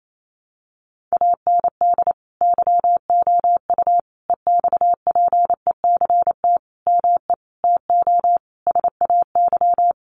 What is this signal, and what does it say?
Morse code, heard as a keyed tone, about 28 wpm, 700 Hz. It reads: AND YOU EXPECT ME TO SAY
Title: Sentences from Top 1100 Words 28wpm